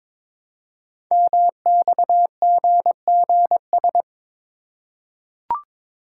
Morse code, heard as a keyed tone, about 23 wpm, 700 Hz. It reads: MXGGS
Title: Code Groups - 5 Letters - 22wpm